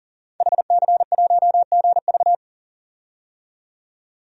Morse code, heard as a keyed tone, about 40 wpm, 700 Hz. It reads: HC1GV
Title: DXCC-100 Call Signs - 5 Characters - 40wpm